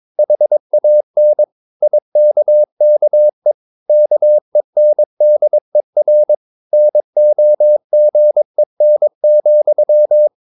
Morse code, heard as a keyed tone, about 22 wpm, 600 Hz. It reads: HAN IKKE KENDER NOGEN,
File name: Ferie QRS 22wpm